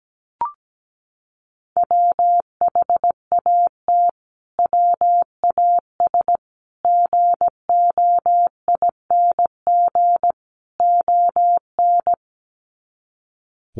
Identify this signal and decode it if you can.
Morse code, heard as a keyed tone, about 17 wpm, 700 Hz. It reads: WHAT WAS GOING ON
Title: Sets of 4 Words 17wpm